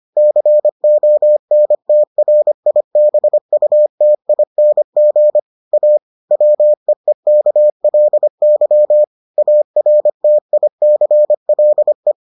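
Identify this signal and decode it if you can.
Morse code, heard as a keyed tone, about 25 wpm, 600 Hz. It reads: CONTRIBUTING A WEEKLY ARTICLE
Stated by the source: Morse edition of the Foundations of Amateur Radio episodes.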